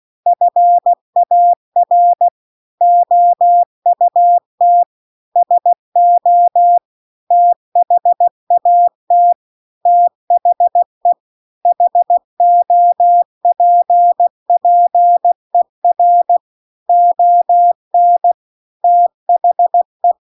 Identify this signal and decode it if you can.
Morse code, heard as a keyed tone, about 16 wpm, 700 Hz. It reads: FAR OUT SO THAT THE HOPPER ON THE